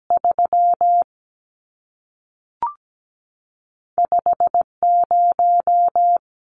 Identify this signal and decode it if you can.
Morse code, heard as a keyed tone, about 17 wpm, 700 Hz. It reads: 3 5 0
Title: Numbers - 2-Digit 17wpm